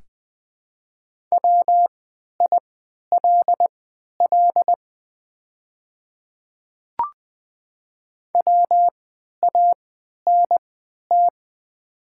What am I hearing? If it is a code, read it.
Morse code, heard as a keyed tone, about 20 wpm, spaced as for 11 wpm, 700 Hz. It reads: WILL WANT
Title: 3x Character Spacing - Top 300 Words - 20wpm